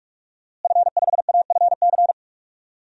Morse code, heard as a keyed tone, about 46 wpm, 700 Hz. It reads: F5AFC
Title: DXCC-100 Call Signs - 5 Characters - Repeated 3x - 45wpm